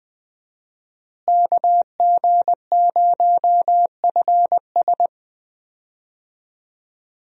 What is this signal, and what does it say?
Morse code, heard as a keyed tone, about 21 wpm, 700 Hz. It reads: KG0FS
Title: USA Call Signs - 5 Characters - Repeated 2x - 20wpm